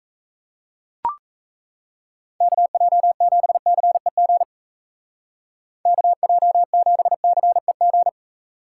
Morse code, heard as a keyed tone, about 34 wpm, 700 Hz. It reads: KJ7CEG KJ7CEG
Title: Speed Racing - USA Call Signs 28wpm